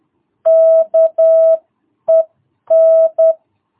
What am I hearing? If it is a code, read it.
Morse code, heard as a keyed tone, about 10 wpm, spaced as for 9 wpm, 660 Hz. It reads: KEN
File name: jh8cbh / 2022cw